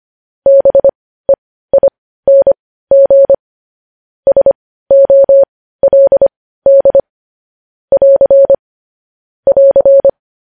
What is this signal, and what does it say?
Morse code, heard as a keyed tone, about 25 wpm, spaced as for 15 wpm, 550 Hz. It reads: BEING SOLD <AR> <AR>
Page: news headlines in morse code at 15 wpm